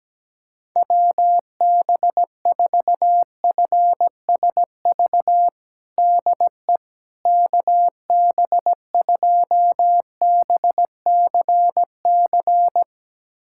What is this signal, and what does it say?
Morse code, heard as a keyed tone, about 17 wpm, 700 Hz. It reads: WB4FSV DE KB2BCC
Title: USA Callsign DE Callsign - 6 Characters - 17wpm